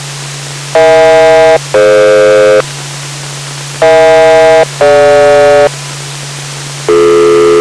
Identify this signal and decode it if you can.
Aircraft SELCAL, heard as two pairs of tones. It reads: EH-DF EH-DG